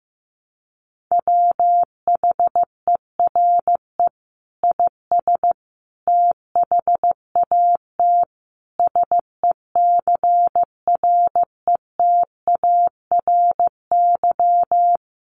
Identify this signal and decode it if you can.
Morse code, heard as a keyed tone, about 15 wpm, 700 Hz. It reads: WHERE IS THAT SECRETARY
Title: Sentences from Top 1300 Words 15wpm